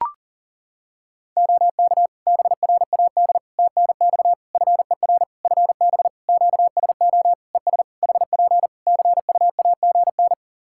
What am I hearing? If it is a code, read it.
Morse code, heard as a keyed tone, about 40 wpm, 700 Hz. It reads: OK BRAD TNX FER FB QSO ES HP CUAGN
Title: K1USN SST QSO Practice 40wpm